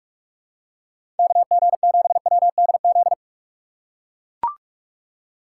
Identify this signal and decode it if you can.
Morse code, heard as a keyed tone, about 42 wpm, 700 Hz. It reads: KG7WDZ
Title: USA Call Signs - 6 Characters - 3x Character Spacing - 45wpm